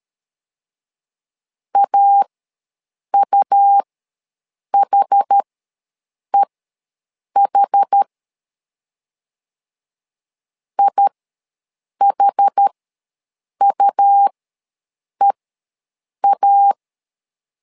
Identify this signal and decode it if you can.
Morse code, heard as a keyed tone, about 13 wpm, spaced as for 7 wpm, 800 Hz. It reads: AUHEH IHUEA
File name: LEZIONI / 2 UHDN / 2 UH ws 12wpm 1730